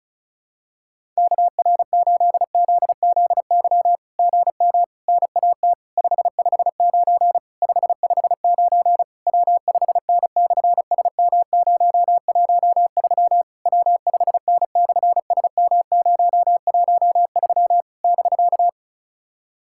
Morse code, heard as a keyed tone, about 36 wpm, 700 Hz. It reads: KR8ZZY GM NAT 559 559 W5N/SM013 W5N/SM013 <BK>
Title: SOTA QSO Practice 35wpm